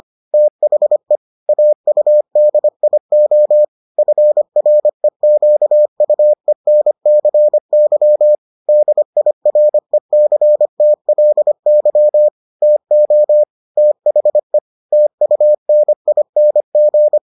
Morse code, heard as a keyed tone, about 25 wpm, 600 Hz. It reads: THE AUDIO FREQUENCY DIRECTLY TO THE TUNING